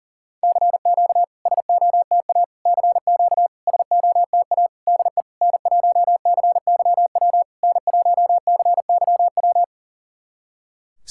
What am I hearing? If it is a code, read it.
Morse code, heard as a keyed tone, about 40 wpm, 690 Hz. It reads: CQ SOTA CQ SOTA DE N1CYW N1CYW
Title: SOTA QSO Practice 40wpm